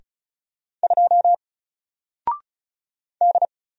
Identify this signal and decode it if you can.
Morse code, heard as a keyed tone, about 34 wpm, 700 Hz. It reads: 2 D
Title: Single Letter-Number Alt 35wpm